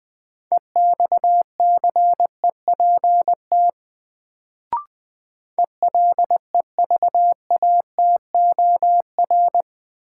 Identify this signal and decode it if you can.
Morse code, heard as a keyed tone, about 20 wpm, 700 Hz. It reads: EXCEPT ELEVATOR
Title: Sentences from Top 1600 Words - New Word Drill - 20wpm